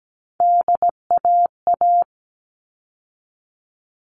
Morse code, heard as a keyed tone, about 17 wpm, 700 Hz. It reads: DAA
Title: All Cut Numbers - 3-Digit 17wpm